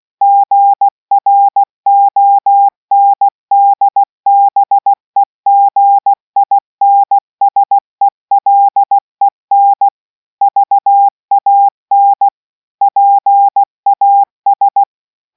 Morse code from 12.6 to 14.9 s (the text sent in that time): PAS